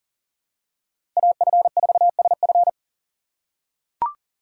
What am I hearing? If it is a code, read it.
Morse code, heard as a keyed tone, about 40 wpm, 700 Hz. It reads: AF4SF
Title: USA Call Signs - 5 Characters - 2x Character Spacing - 40wpm